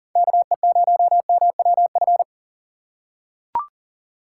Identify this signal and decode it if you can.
Morse code, heard as a keyed tone, about 38 wpm, 700 Hz. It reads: KE0MWF
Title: USA Call Signs - 6 Characters - 2x Character Spacing - 40wpm